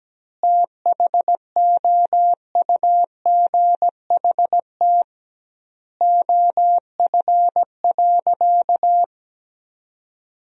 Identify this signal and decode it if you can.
Morse code, heard as a keyed tone, about 17 wpm, 700 Hz. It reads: THOUGHT OF.
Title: 2x Word-Spacing - Sentences from Top 300 Words - 17wpm